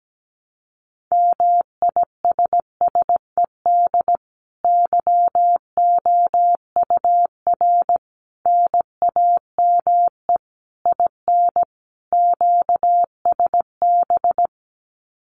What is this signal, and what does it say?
Morse code, heard as a keyed tone, about 17 wpm, 700 Hz. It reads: MISSED YOUR NAME IN QSB
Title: Common QSO Phrases 17wpm